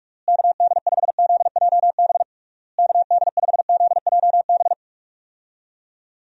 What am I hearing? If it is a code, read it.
Morse code, heard as a keyed tone, about 44 wpm, 700 Hz. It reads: KD5ZJB KD5ZJB
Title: USA Call Signs - 6 Characters - Repeated 3x - 45wpm